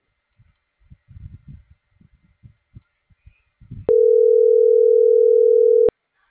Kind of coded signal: telephone call progress tone

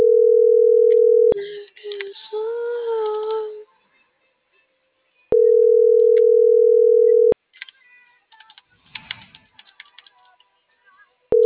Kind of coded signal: telephone call progress tone

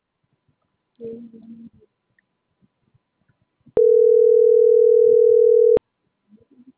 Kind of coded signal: telephone call progress tone

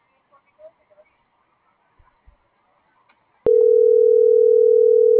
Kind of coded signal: telephone call progress tone

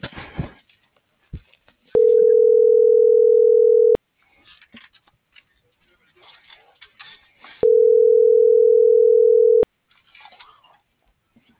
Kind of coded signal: telephone call progress tone

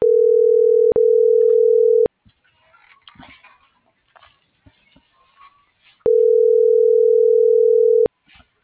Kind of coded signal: telephone call progress tone

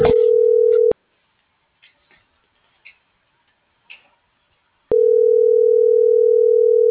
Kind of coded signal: telephone call progress tone

